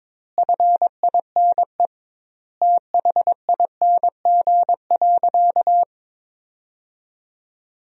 Morse code, heard as a keyed tone, about 21 wpm, 700 Hz. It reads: FINE THING.
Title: 2x Word-Spacing - Sentences from Top 500 Words - 22wpm